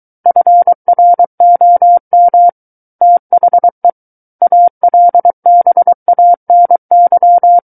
Morse code, heard as a keyed tone, about 23 wpm, 700 Hz. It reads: FROM THE ALBANY
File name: Book One / 14-Chapter 14 - 23 WPM